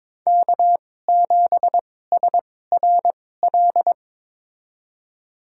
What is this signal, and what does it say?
Morse code, heard as a keyed tone, about 22 wpm, spaced as for 16 wpm, 700 Hz. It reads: K7SRL
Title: USA Call Signs - 5 Characters - 2x Character Spacing - 22wpm